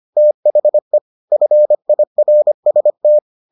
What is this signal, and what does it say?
Morse code, heard as a keyed tone, about 25 wpm, 600 Hz. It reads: THE FIRST